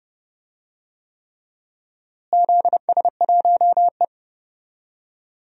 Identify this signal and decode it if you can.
Morse code, heard as a keyed tone, about 29 wpm, 700 Hz. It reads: ZS1E
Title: DXCC-100 Call Signs - 4 Characters - Repeated 3x - 30wpm